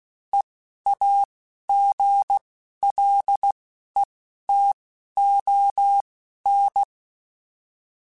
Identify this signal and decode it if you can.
Morse code, heard as a keyed tone, about 16 wpm, spaced as for 11 wpm, 800 Hz. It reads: EAGLETON